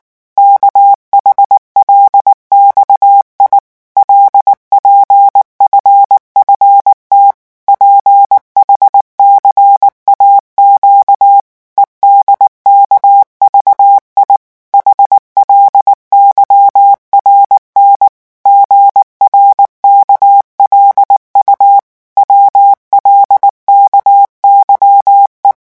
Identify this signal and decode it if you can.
Morse code, heard as a keyed tone, about 19 wpm, 800 Hz. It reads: KHLXI LPFFT PHCAQ EDKVI HLYRN GRKLU WLKYE
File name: morse mp3 / revision7-vitesse 18 mots